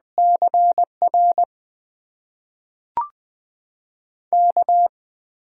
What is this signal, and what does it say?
Morse code, heard as a keyed tone, about 20 wpm, spaced as for 5 wpm, 700 Hz. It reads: CRK
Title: USA Call Signs - 6 Characters - 8x Character Spacing - 20wpm